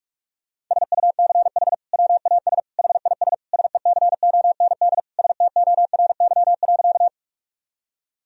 Morse code, heard as a keyed tone, about 46 wpm, 700 Hz. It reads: SUCH WAS HIS SECOND STORY.
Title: Sentences from Top 1000 Words - Review Part 1 - 45wpm